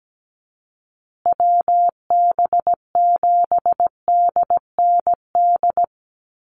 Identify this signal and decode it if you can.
Morse code, heard as a keyed tone, about 17 wpm, 700 Hz. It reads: WB7DND
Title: Call Signs Repeated 2x - 17wpm